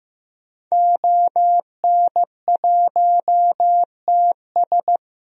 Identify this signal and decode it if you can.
Morse code, heard as a keyed tone, about 15 wpm, 700 Hz. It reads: ON1TS
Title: DXCC-100 Call Signs - 5 Characters - Repeated 2x - 15wpm